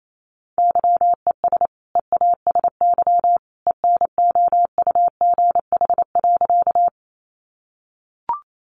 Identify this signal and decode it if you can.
Morse code, heard as a keyed tone, about 28 wpm, 700 Hz. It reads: YES EASY ENOUGH.
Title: Sentences from Top 700 Words 28wpm